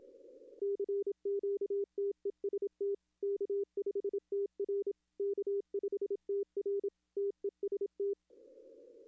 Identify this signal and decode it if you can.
Morse code, heard as a keyed tone, about 27 wpm, 400 Hz. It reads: CQTEST K5TR K5TR TEST
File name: contests / arrl-160-k5tr-op-w5tn -1300